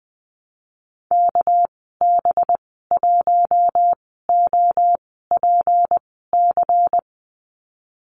Morse code, heard as a keyed tone, about 20 wpm, spaced as for 15 wpm, 700 Hz. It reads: KB1OPC
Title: 219 Lesson 35g - Number 0 - Call Signs 20wpm